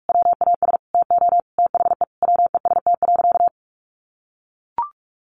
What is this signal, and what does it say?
Morse code, heard as a keyed tone, about 44 wpm, 700 Hz. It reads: WAS TO THE WEST.
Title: Sentences from Top 1200 Words 45wpm